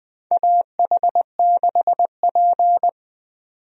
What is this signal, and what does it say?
Morse code, heard as a keyed tone, about 20 wpm, 700 Hz. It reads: AH6P
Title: USA Call Signs - 4 Characters - 20wpm